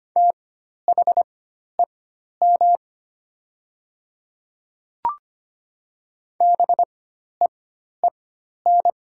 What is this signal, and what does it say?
Morse code, heard as a keyed tone, about 25 wpm, spaced as for 12 wpm, 690 Hz. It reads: THEM BEEN
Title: 4x Character Spacing - Top 100 Words - 25wpm